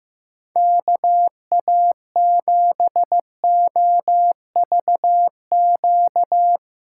Morse code, heard as a keyed tone, about 15 wpm, 700 Hz. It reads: KA7OVQ
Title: USA Call Signs - 6 Characters - 8x Character Spacing - 15wpm